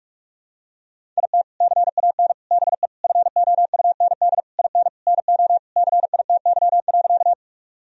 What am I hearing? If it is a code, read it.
Morse code, heard as a keyed tone, about 46 wpm, 700 Hz. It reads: IT CAN BE FOUND IN NO CITY.
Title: Sentences from Top 1000 Words - Review Part 1 - 45wpm